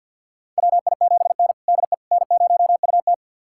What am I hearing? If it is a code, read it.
Morse code, heard as a keyed tone, about 52 wpm, 700 Hz. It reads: WI7N DE N0UT
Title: USA Callsign DE Callsign - 4 Characters - 50wpm